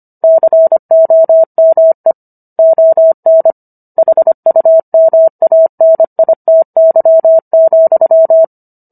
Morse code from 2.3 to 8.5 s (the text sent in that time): ON HUMANITY,